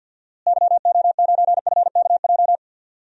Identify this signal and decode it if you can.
Morse code, heard as a keyed tone, about 49 wpm, 700 Hz. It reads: YO9FKJ
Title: DXCC-100 Call Signs - 6 Characters - Repeated 2x - 50wpm